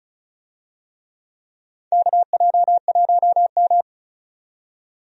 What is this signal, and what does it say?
Morse code, heard as a keyed tone, about 35 wpm, 700 Hz. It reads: KJ1M
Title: USA Call Signs - 4 Characters - 35wpm